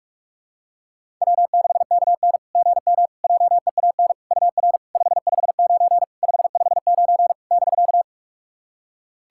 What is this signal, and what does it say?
Morse code, heard as a keyed tone, about 47 wpm, 690 Hz. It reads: W6KN GM JEAN UR 559 559 <BK>
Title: SOTA QSO Practice 45wpm